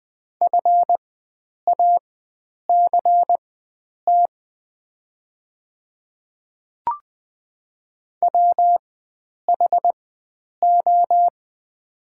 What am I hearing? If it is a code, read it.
Morse code, heard as a keyed tone, about 20 wpm, spaced as for 9 wpm, 700 Hz. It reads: FACT WHO
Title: 4x Character Spacing - Top 500 Words - 20wpm